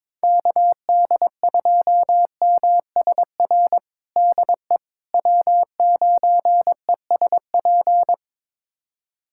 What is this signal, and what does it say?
Morse code, heard as a keyed tone, about 22 wpm, 700 Hz. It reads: KD2MSR DE W9ESP
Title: USA Callsign DE Callsign 22wpm